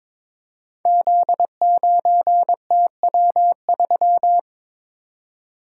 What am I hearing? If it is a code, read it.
Morse code, heard as a keyed tone, about 22 wpm, 700 Hz. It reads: Z9TW3
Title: Code Groups - 5 Letter:Numbers - 22wpm